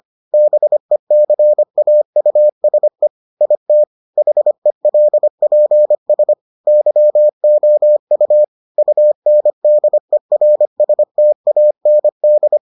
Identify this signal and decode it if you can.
Morse code, heard as a keyed tone, about 25 wpm, 600 Hz. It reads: BECAUSE IT HELPS YOU UNDERSTAND